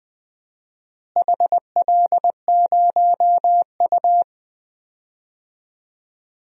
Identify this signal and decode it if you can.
Morse code, heard as a keyed tone, about 20 wpm, 700 Hz. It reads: HL0U
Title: DXCC-100 Call Signs - 4 Characters - Repeated 2x - 20wpm